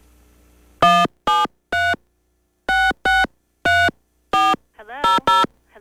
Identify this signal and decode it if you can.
DTMF, heard as telephone touch tones, tones of about 220 ms. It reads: 173 663400